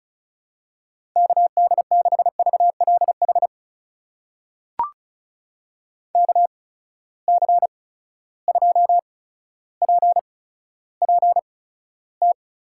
Morse code, heard as a keyed tone, about 36 wpm, spaced as for 15 wpm, 700 Hz. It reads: KD6VLH KC2PPT
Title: USA Call Signs - 6 Characters - 8x Character Spacing - 35wpm